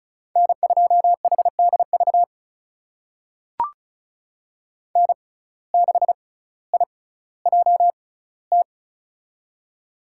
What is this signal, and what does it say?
Morse code, heard as a keyed tone, about 36 wpm, spaced as for 18 wpm, 700 Hz. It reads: N2HDV N6IJT
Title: USA Call Signs - 5 Characters - 6x Character Spacing - 35wpm